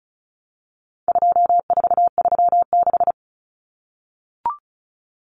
Morse code, heard as a keyed tone, about 34 wpm, 700 Hz. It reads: 2436
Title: Numbers - 4-Digit 35wpm